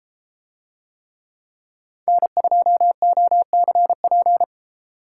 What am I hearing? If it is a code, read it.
Morse code, heard as a keyed tone, about 32 wpm, 700 Hz. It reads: N2OCP